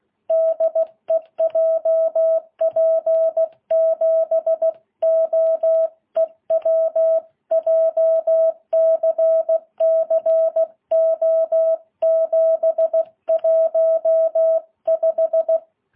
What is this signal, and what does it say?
Morse code, heard as a keyed tone, about 16 wpm, 660 Hz. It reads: DE JP7OEW JCC O715
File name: jh8cbh